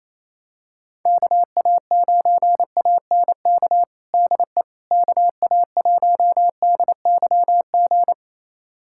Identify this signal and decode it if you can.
Morse code, heard as a keyed tone, about 28 wpm, 700 Hz. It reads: KA9ANK DE KA1DYG
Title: USA Callsign DE Callsign - 6 Characters - 28wpm